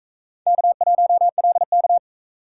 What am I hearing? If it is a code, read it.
Morse code, heard as a keyed tone, about 42 wpm, 700 Hz. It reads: K1LK